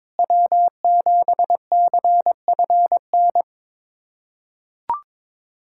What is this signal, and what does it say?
Morse code, heard as a keyed tone, about 21 wpm, 700 Hz. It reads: W7CFN
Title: USA Callsign DE Callsign - 5 Characters - 22wpm